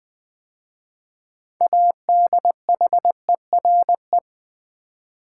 Morse code, heard as a keyed tone, about 21 wpm, 700 Hz. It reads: ADHERE
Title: 6 Letter Words - First 3 Letters Given - 20wpm